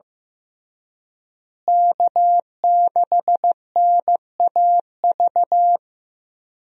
Morse code, heard as a keyed tone, about 15 wpm, 700 Hz. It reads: K6NAV
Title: USA Call Signs - 5 Characters - Repeated 3x - 15wpm